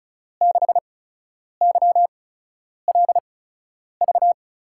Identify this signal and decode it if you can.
Morse code, heard as a keyed tone, about 36 wpm, spaced as for 10 wpm, 700 Hz. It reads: 6YLV
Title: USA Call Signs - 5 Characters - 8x Character Spacing - 35wpm